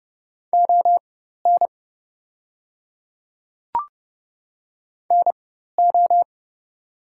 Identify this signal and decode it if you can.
Morse code, heard as a keyed tone, about 30 wpm, spaced as for 14 wpm, 700 Hz. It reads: ON NO
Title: 4x Character Spacing - Top 100 Words - 30wpm